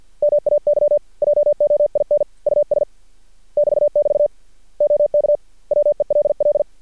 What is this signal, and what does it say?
Morse code, heard as a keyed tone, about 48 wpm, 590 Hz. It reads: MAY JOIN US <BT><BT> OK WELL
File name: archived / CRSnet